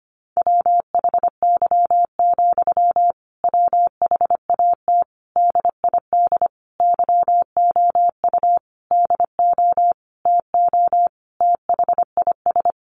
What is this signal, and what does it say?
Morse code, heard as a keyed tone, about 25 wpm, 700 Hz. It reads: WHY, WHAT DID YOU DO TO THIS
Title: Sentences from Top 1500 Words 25wpm